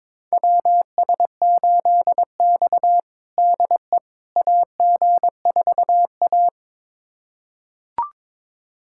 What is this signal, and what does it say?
Morse code, heard as a keyed tone, about 22 wpm, 700 Hz. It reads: WS8X DE AG4A
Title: USA Callsign DE Callsign - 4 Characters - 22wpm